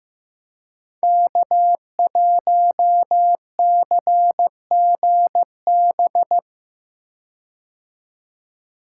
Morse code, heard as a keyed tone, about 15 wpm, 700 Hz. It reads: K1CGB
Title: Code Groups - 5 Letter:Numbers - 15wpm